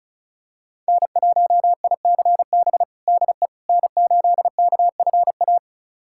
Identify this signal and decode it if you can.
Morse code, heard as a keyed tone, about 36 wpm, 700 Hz. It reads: N1ICB DE N8KFA
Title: USA Callsign DE Callsign - 5 Characters - 35wpm